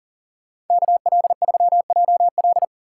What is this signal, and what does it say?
Morse code, heard as a keyed tone, about 52 wpm, 700 Hz. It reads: KL3JL